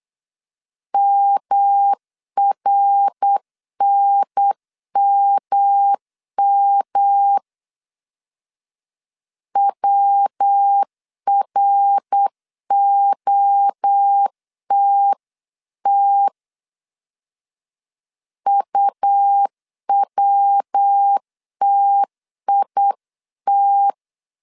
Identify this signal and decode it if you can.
Morse code, heard as a keyed tone, about 8 wpm, 800 Hz. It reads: MRNMM WROTT UWTIT